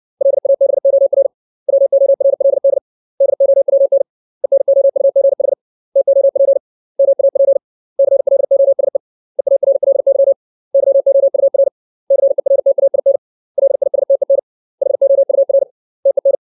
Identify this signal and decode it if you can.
Morse code, heard as a keyed tone, about 60 wpm, 550 Hz. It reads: LABOR WORLD DOWN ENOUGH TOP MAP CLOSE EARLY YOUR CERTAIN LISTEN HOUR TEN
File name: 632 words / 632 Words 60WPM